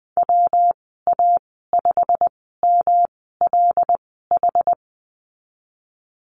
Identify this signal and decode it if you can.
Morse code, heard as a keyed tone, about 20 wpm, spaced as for 15 wpm, 700 Hz. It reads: WA5MLH
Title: 128 Lesson 21g - Number 6 - Call Signs 20wpm